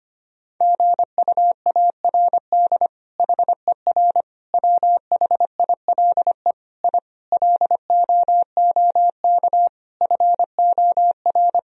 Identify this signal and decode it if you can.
Morse code, heard as a keyed tone, about 25 wpm, 700 Hz. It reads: GUARD HER WHILE I LOOK FOR
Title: Sentences from Top 1000 Words - Review Part 4 - 25wpm